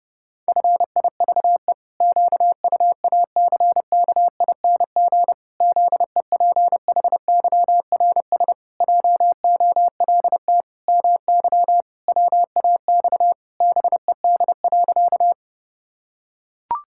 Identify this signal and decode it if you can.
Morse code, heard as a keyed tone, about 32 wpm, 700 Hz. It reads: FIVE QUACKING ZEPHYRS JOLT MY WAX BED.
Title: Pangram Challenge v2 - 30wpm